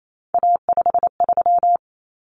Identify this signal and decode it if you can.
Morse code, heard as a keyed tone, about 28 wpm, 700 Hz. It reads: A53
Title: Common Cut Numbers - 3-Digit 28wpm